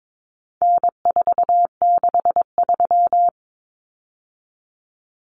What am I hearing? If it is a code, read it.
Morse code, heard as a keyed tone, about 22 wpm, 700 Hz. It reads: N463
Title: Common Cut Numbers - 4-Digit 22wpm